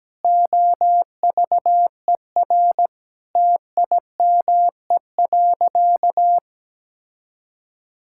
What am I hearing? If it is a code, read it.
Morse code, heard as a keyed tone, about 17 wpm, 700 Hz. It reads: OVER TIME.
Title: Sentences from Top 1000 Words - Review Part 6 - 17wpm